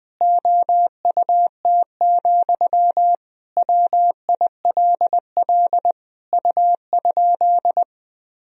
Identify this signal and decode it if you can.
Morse code, heard as a keyed tone, about 20 wpm, 700 Hz. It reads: OUT, WILL U?
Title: Sentences from Top 300 Words and Abbreviations 20wpm